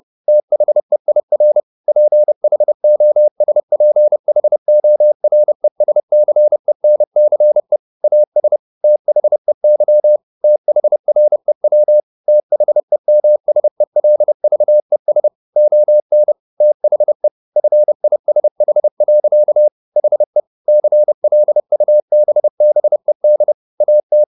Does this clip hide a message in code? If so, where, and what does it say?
Yes: Morse code from 0.3 to 24.2 s, heard as a keyed tone, about 30 wpm, 600 Hz. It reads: THEIR PHOSPHORESCENCE AS THEY THREW THEMSELVES ON THE FISH. HE CLUBBED AT